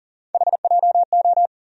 Morse code, heard as a keyed tone, about 40 wpm, 700 Hz. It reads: HJO
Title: USA Call Sign Postfixes - 3 Letters 40wpm